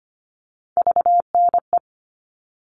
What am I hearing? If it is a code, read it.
Morse code, heard as a keyed tone, about 25 wpm, 700 Hz. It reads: VNE
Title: All Cut Numbers - 3-Digit 25wpm